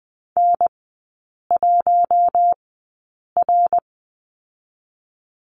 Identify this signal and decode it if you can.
Morse code, heard as a keyed tone, about 20 wpm, spaced as for 8 wpm, 700 Hz. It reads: N1R
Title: 053 Lesson 10e - Letter r - Three Characters 20wpm